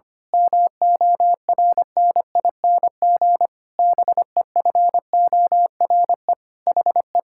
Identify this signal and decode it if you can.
Morse code, heard as a keyed tone, about 25 wpm, 700 Hz. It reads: MORNING BEFORE HE
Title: Pangram Challenge v2 - 25wpm